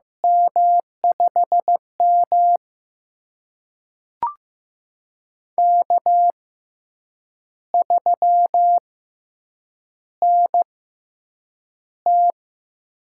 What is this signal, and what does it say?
Morse code, heard as a keyed tone, about 15 wpm, spaced as for 7 wpm, 700 Hz. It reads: M5M K3NT
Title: USA Call Signs - 6x Character Spacing - 15wpm